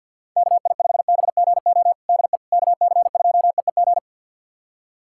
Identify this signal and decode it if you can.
Morse code, heard as a keyed tone, about 48 wpm, 700 Hz. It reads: KI5BGO DE KC2EEG
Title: USA Callsign DE Callsign 50wpm